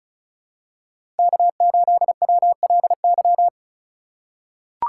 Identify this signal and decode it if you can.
Morse code, heard as a keyed tone, about 36 wpm, 700 Hz. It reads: K8WLY
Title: USA Call Signs - 5 Characters - 2x Character Spacing - 35wpm